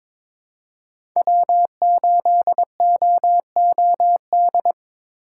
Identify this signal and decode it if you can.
Morse code, heard as a keyed tone, about 22 wpm, 700 Hz. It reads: W8OOD
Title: USA Call Signs - 5 Characters - 4x Character Spacing - 22wpm